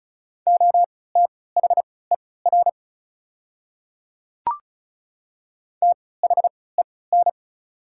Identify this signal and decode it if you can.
Morse code, heard as a keyed tone, about 36 wpm, spaced as for 20 wpm, 690 Hz. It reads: OTHER THEN
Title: 3x Character Spacing - Top 200 Words - 35wpm